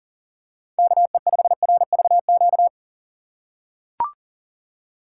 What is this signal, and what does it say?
Morse code, heard as a keyed tone, about 40 wpm, 700 Hz. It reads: KE5RVQ
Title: USA Call Signs - 6 Characters - 3x Character Spacing - 40wpm